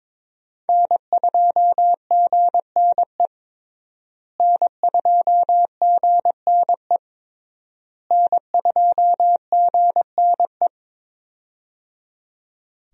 Morse code, heard as a keyed tone, about 21 wpm, 700 Hz. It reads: N2GNE N2GNE N2GNE
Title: USA Call Signs - 5 Characters - Repeated 3x - 22wpm